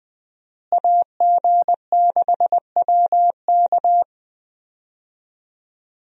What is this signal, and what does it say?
Morse code, heard as a keyed tone, about 21 wpm, 700 Hz. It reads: AG6WK
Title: USA Call Signs - 5 Characters - Repeated 3x - 20wpm